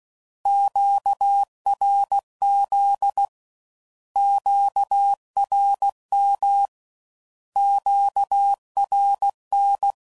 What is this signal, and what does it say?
Morse code, heard as a keyed tone, about 16 wpm, 800 Hz. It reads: QRZ QRM QRN
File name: Mixed / Misc / Words